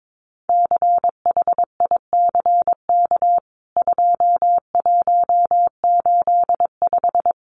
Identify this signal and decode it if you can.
Morse code, heard as a keyed tone, about 22 wpm, 700 Hz. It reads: CHICK 2185
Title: CWT Exchange 22wpm